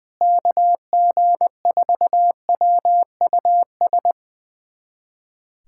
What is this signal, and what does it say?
Morse code, heard as a keyed tone, about 20 wpm, 700 Hz. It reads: KG4WUS
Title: USA Callsign DE Callsign - 6 Characters - 20wpm